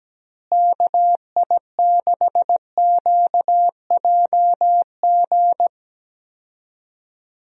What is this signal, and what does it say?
Morse code, heard as a keyed tone, about 17 wpm, 700 Hz. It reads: KI6QJG
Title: USA Call Signs - 6 Characters - Repeated 2x - 17wpm